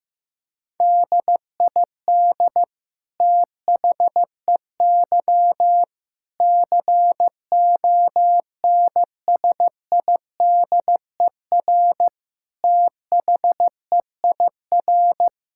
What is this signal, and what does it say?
Morse code, heard as a keyed tone, about 15 wpm, 700 Hz. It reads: DID THEY CONSIDER THEIR
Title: Sentences from Top 1000 Words - Review Part 6 - 15wpm